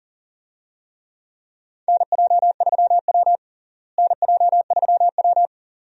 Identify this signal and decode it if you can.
Morse code, heard as a keyed tone, about 38 wpm, 700 Hz. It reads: NJ3W NJ3W
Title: USA Call Signs - 4 Characters - Repeated 2x - 40wpm